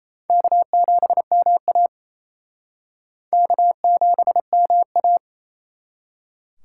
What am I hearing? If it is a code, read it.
Morse code, heard as a keyed tone, about 34 wpm, 700 Hz. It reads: K7MA K7MA
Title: Speed Racing - USA 1x2 Call Signs 28wpm